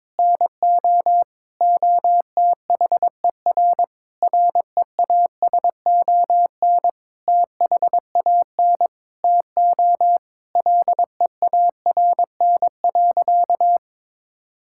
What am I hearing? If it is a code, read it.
Morse code, heard as a keyed tone, about 21 wpm, 700 Hz. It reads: NO OTHER REASON THAN TO LEARN.
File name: Sentences with 7-Word Phrases - 22wpm